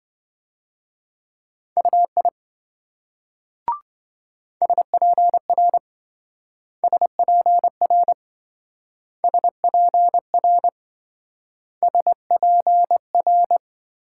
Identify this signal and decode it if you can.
Morse code, heard as a keyed tone, about 26 wpm, 700 Hz. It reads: UI SPR SPR SPR SPR